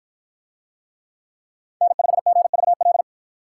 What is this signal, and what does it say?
Morse code, heard as a keyed tone, about 53 wpm, 700 Hz. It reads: N5GVL